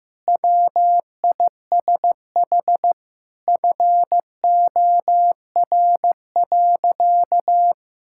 Morse code, heard as a keyed tone, about 15 wpm, 700 Hz. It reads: WISH FOR.
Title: Sentences from Top 1600 Words 15wpm